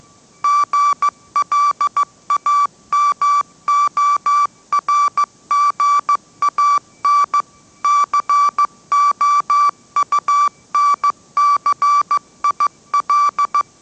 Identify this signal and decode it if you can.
Morse code, heard as a keyed tone, about 15 wpm, 1160 Hz. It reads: GLAMORGAN COUNCIL
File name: mrx / mrx066 16wpm